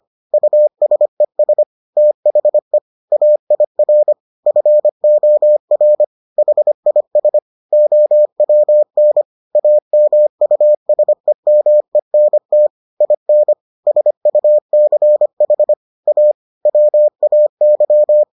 Morse code, heard as a keyed tone, about 25 wpm, 600 Hz. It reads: USES THE AIR FOR HIS OWN AMUSEMENT IN SUCH A WAY